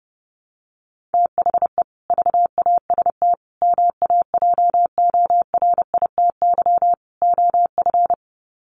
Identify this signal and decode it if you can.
Morse code, heard as a keyed tone, about 30 wpm, 700 Hz. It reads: THE VAST MAJORITY OF
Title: Sets of 4 Words 30wpm